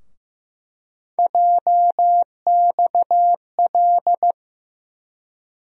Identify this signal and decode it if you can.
Morse code, heard as a keyed tone, about 15 wpm, 700 Hz. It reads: JXL
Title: USA Call Sign Postfixes - 3 Letters 15wpm